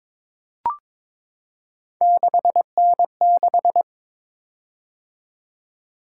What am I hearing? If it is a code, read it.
Morse code, heard as a keyed tone, about 22 wpm, 700 Hz. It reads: 6N6
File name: practice / 22wpm / Three Letter-Number Combinations - 22wpm